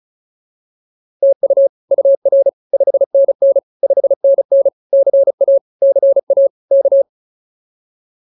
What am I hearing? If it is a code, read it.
Morse code, heard as a keyed tone, about 34 wpm, 550 Hz. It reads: TU UR 5NN 5NN CA CA K